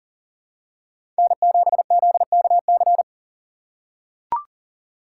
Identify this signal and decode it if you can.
Morse code, heard as a keyed tone, about 40 wpm, 700 Hz. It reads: N7ZKC
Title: USA Call Signs - 4x Character Spacing - 40wpm